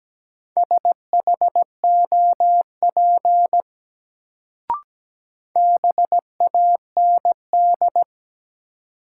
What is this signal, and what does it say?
Morse code, heard as a keyed tone, about 17 wpm, 700 Hz. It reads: SHOP BAND
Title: Sentences from Top 1600 Words - New Word Drill Alt - 17wpm